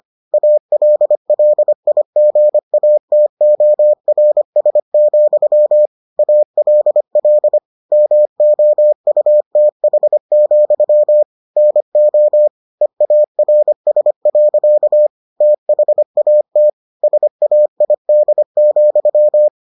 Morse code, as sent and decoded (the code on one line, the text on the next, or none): .- .-.. .-.. .. --. .- - --- .-. ... --..-- / .- .-.. .-.. / -- --- ..- - .... --..-- / -. --- / . .- .-. ... .-.-.- / - .... .- - / ... .- .. -.. --..--
ALLIGATORS, ALL MOUTH, NO EARS. THAT SAID,